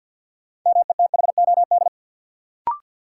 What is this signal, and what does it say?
Morse code, heard as a keyed tone, about 52 wpm, 700 Hz. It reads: METHOD
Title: Sentences from Top 600 Words - New Word Drill - 50wpm